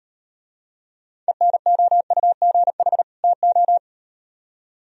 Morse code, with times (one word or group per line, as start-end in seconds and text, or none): 1.3-3.0 ENOUGH
3.2-3.8 TO